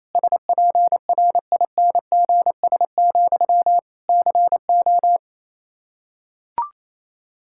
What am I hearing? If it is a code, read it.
Morse code, heard as a keyed tone, about 28 wpm, 700 Hz. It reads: SPRINGS, CO
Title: Practice QSO - Exchange 1 - Beginning - 28wpm